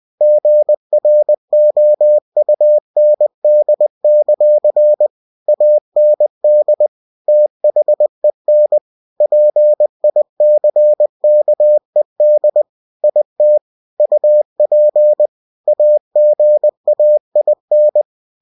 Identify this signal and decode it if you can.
Morse code, heard as a keyed tone, about 20 wpm, 600 Hz. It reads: GROUND; AND THEN PICKED IT UP AGAIN
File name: cw-books / winnie-the-pooh / winnie-20wpm-page-0058